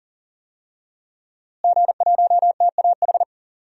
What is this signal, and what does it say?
Morse code, heard as a keyed tone, about 40 wpm, 700 Hz. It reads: G1TAH
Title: DXCC-100 Call Signs - 5 Characters - Repeated 3x - 40wpm